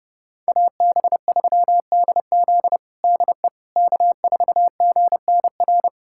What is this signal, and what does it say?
Morse code, heard as a keyed tone, about 32 wpm, 700 Hz. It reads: AB3DZ DE K4GNR
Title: USA Callsign DE Callsign - 5 Characters - 30wpm